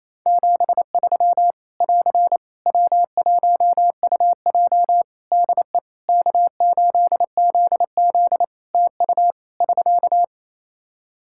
Morse code, heard as a keyed tone, about 28 wpm, 700 Hz. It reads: 73 <AR> W1UJ DE K8ZZ TU <SK>
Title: Practice QSO - Exchange 4 - Ending - 28wpm